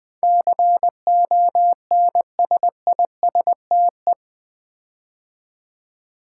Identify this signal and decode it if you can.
Morse code, heard as a keyed tone, about 20 wpm, 700 Hz. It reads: CONSISTE
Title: Build A Word - 4 Letters At A Time - 12-Letter Words - 20wpm